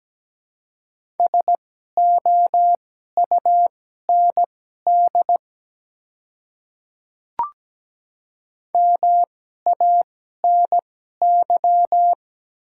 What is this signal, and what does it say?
Morse code, heard as a keyed tone, about 17 wpm, spaced as for 12 wpm, 700 Hz. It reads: SOUND MANY
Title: 2x Character Spacing - Top 100 Words - 17wpm